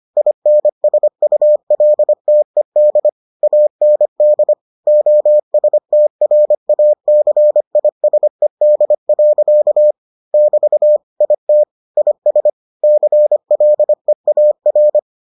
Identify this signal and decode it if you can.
Morse code, heard as a keyed tone, about 25 wpm, 600 Hz. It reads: INSULTED AND OSTRACISED. <BT> IT IS CLEAR